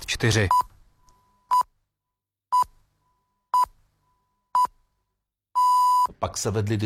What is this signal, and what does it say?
Radio time signal (time pips), with speech.